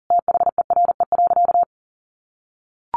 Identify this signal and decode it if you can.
Morse code, heard as a keyed tone, about 40 wpm, 700 Hz. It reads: THERE.
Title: Sentences from Top 300 Words 40wpm